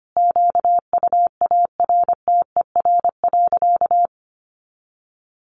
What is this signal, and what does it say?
Morse code, heard as a keyed tone, about 25 wpm, 700 Hz. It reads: QUARTER.
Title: Sentences from Top 1500 Words 25wpm